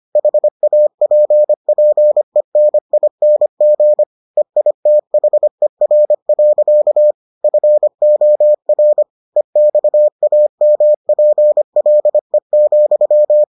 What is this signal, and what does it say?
Morse code, heard as a keyed tone, about 25 wpm, 600 Hz. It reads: HAPPENING EITHER. FOR EXAMPLE,